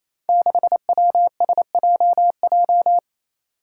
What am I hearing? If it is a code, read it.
Morse code, heard as a keyed tone, about 28 wpm, 700 Hz. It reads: 6WSJJ
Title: Five Challenging Letter-Number Combinations 28wpm